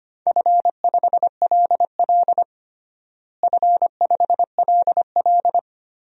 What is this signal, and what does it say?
Morse code, heard as a keyed tone, about 25 wpm, 700 Hz. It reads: F5LL F5LL
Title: DXCC-100 Call Signs - 4 Characters - Repeated 2x - 25wpm